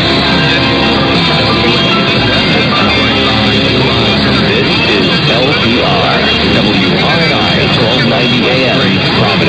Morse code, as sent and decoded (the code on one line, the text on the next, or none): none